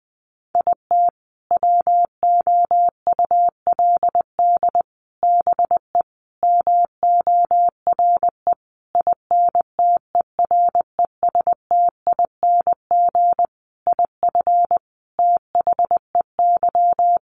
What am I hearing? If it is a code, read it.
Morse code, heard as a keyed tone, about 20 wpm, 700 Hz. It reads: IT WOULD BE MORE INTERESTING IF THEY
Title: Sentences from Top 1000 Words 20wpm